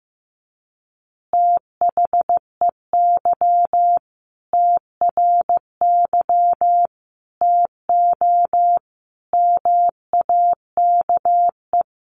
Morse code, heard as a keyed tone, about 15 wpm, 700 Hz. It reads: THEY TRY TO MAKE
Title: Easy Phrases 15wpm